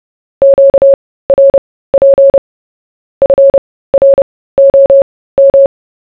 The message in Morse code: QRP FROM